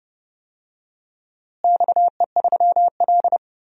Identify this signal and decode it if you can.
Morse code, heard as a keyed tone, about 32 wpm, 700 Hz. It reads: XE3L
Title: DXCC-100 Call Signs - 4 Characters - Repeated 3x - 30wpm